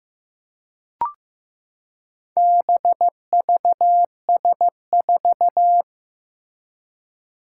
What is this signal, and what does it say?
Morse code, heard as a keyed tone, about 15 wpm, 700 Hz. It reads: BVS4
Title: Four Challenging Letter-Number Combinations 15wpm